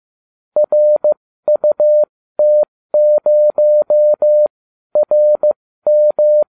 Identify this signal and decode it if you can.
Morse code, heard as a keyed tone, about 15 wpm, spaced as for 12 wpm, 600 Hz. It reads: RUT0RM